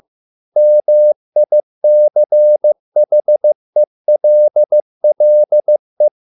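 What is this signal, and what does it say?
Morse code, heard as a keyed tone, about 15 wpm, 600 Hz. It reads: MICHELLE
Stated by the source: Morse code at a slow 15 wpm from CW med Gnister.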